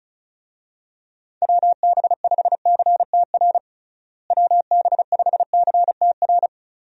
Morse code, heard as a keyed tone, about 36 wpm, 700 Hz. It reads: WB5CTR WB5CTR
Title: USA Call Signs - 6 Characters - Repeated 2x - 35wpm